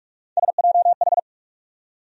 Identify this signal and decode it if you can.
Morse code, heard as a keyed tone, about 42 wpm, 700 Hz. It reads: SJH
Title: USA Call Sign Postfixes - 3 Letters 45wpm